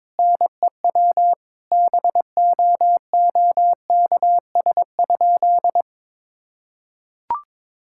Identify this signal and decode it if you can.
Morse code, heard as a keyed tone, about 22 wpm, 700 Hz. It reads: NEW BOOKS?
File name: Sentences with 9-Word Phrases - 22wpm